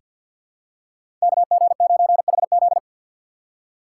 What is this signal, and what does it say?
Morse code, heard as a keyed tone, about 50 wpm, 700 Hz. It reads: KG9HZ
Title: USA Call Signs - 5 Characters - 6x Character Spacing - 50wpm